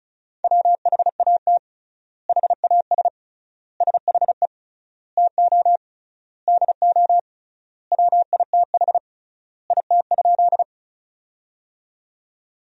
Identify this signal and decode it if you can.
Morse code, heard as a keyed tone, about 36 wpm, 700 Hz. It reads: WHAT HAS SHE TO DO WITH IT?
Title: 3x Word-Spacing - Sentences from Top 100 Words - 35wpm